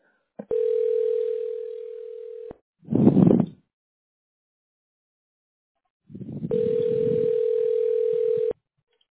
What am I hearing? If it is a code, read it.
Ringback tone (North America), 440 + 480 Hz.